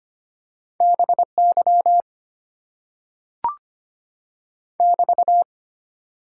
Morse code, heard as a keyed tone, about 25 wpm, 700 Hz. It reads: BY <BT>
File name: Top 90 QSO Elements - 8x Character Spacing - 25wpm